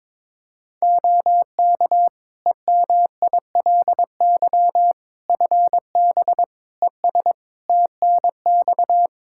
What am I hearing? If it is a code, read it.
Morse code, heard as a keyed tone, about 22 wpm, 700 Hz. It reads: OK EMILY FB ES TNX
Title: Practice QSO - Exchange 2 - Info - 22wpm